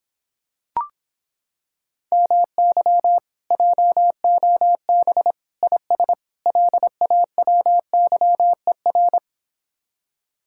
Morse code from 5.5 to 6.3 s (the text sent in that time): IS